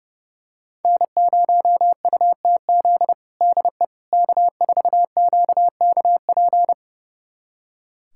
Morse code, heard as a keyed tone, about 30 wpm, 700 Hz. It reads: N0UTZ DE K4QKP
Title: USA Callsign DE Callsign - 5 Characters - 30wpm